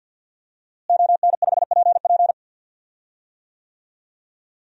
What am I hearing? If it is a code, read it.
Morse code, heard as a keyed tone, about 52 wpm, 700 Hz. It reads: ON5PP
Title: DXCC-100 Call Signs - 5 Characters - Repeated 2x - 50wpm